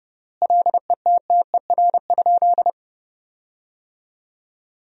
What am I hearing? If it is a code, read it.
Morse code, heard as a keyed tone, about 31 wpm, 700 Hz. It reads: LETTER?
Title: 3x Word-Spacing - Sentences from Top 300 Words - 30wpm